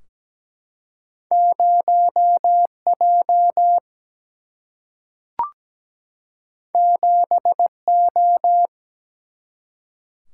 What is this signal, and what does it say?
Morse code, heard as a keyed tone, about 17 wpm, 700 Hz. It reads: 0 J 7 O
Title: Single-Letter Single-Number 17wpm